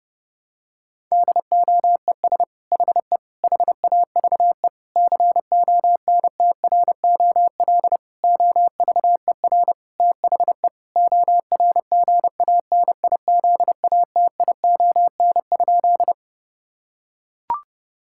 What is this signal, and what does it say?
Morse code, heard as a keyed tone, about 30 wpm, 700 Hz. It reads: DOES HE HAVE CONTROL OVER THE ORGANIZATION?
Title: Sentences from Top 1000 Words - Review Part 5 - 30wpm